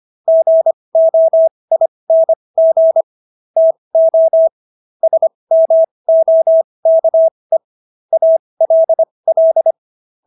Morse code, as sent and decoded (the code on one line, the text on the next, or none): --. --- .. -. --. / - --- / ... -- --- -.- . / .- .-.. .-..
GOING TO SMOKE ALL